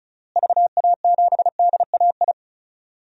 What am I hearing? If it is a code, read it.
Morse code, heard as a keyed tone, about 36 wpm, 700 Hz. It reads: VA7DAI
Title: DXCC-100 Call Signs - 6 Characters - Repeated 3x - 35wpm